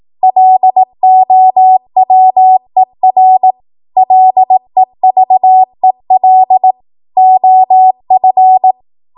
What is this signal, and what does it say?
Morse code, heard as a keyed tone, about 18 wpm, 750 Hz. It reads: LOWER LEVEL OF
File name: Morse / 18WPM